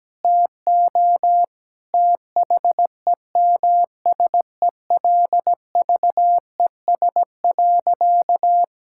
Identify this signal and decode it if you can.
Morse code, heard as a keyed tone, about 17 wpm, 700 Hz. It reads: TO THEMSELVES.
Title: Sentences from Top 1000 Words - Review Part 5 - 17wpm